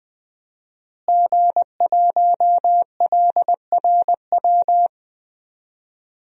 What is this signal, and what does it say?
Morse code, heard as a keyed tone, about 21 wpm, 700 Hz. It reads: G1LRW
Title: DXCC-100 Call Signs - 5 Characters - Repeated 3x - 20wpm